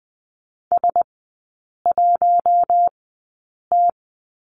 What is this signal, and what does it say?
Morse code, heard as a keyed tone, about 20 wpm, spaced as for 8 wpm, 700 Hz. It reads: S1T
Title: 041 Lesson 8e - Number 1 - Three Characters 20wpm